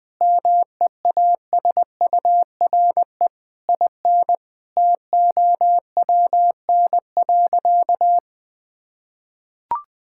Morse code, heard as a keyed tone, about 20 wpm, 700 Hz. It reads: MEASURE IN TOWN.
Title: Sentences from Top 1000 Words - Review Part 5 - 20wpm